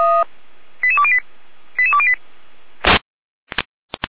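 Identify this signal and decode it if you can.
Five-tone selective call (ZVEI), heard as quick runs of five tones, tones of about 70 ms. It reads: 88208 88208
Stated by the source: ZVEI five-tone call received by SDR, from feuerwehr-orlamuende fire brigade server.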